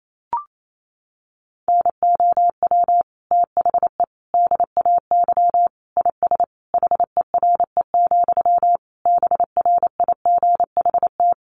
Morse code, heard as a keyed tone, about 28 wpm, 700 Hz. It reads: NOW THE DAY IS HERE, BRIGHT
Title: Sentences from Top 1000 Words 28wpm